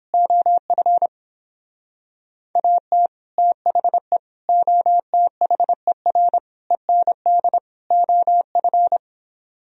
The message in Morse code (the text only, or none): OF AT THE OTHER END OF